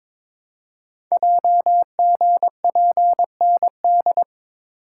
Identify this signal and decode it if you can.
Morse code, heard as a keyed tone, about 22 wpm, 700 Hz. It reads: JGPND
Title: Code Groups - 5 Letters - 22wpm